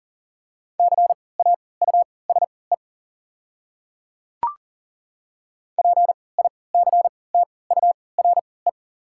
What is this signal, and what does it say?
Morse code, heard as a keyed tone, about 40 wpm, spaced as for 23 wpm, 700 Hz. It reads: CAUSE PICTURE
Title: 3x Character Spacing - Top 300 Words - 40wpm